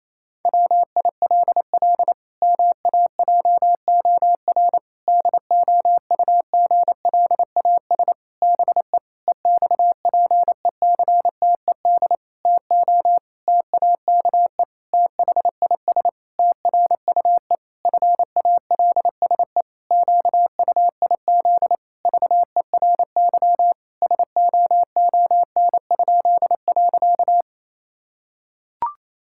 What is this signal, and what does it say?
Morse code, heard as a keyed tone, about 28 wpm, 700 Hz. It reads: WILL MAJOR DOUGLAS BE EXPECTED TO TAKE THIS TRUE FALSE QUIZ VERY SOON?.
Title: Pangram Challenge v2 - 28wpm